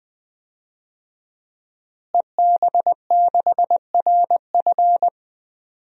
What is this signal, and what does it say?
Morse code, heard as a keyed tone, about 20 wpm, 700 Hz. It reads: EB6RF
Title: DXCC-100 Call Signs - 5 Characters - Repeated 3x - 20wpm